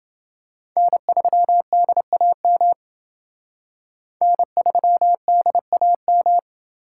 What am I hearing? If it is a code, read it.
Morse code, heard as a keyed tone, about 29 wpm, 700 Hz. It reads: N3DAM N3DAM